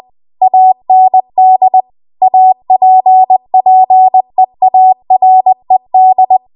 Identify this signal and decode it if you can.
Morse code, heard as a keyed tone, about 20 wpm, 750 Hz. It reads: AND APPEARED
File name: Archive / 20 WPM / 20WPM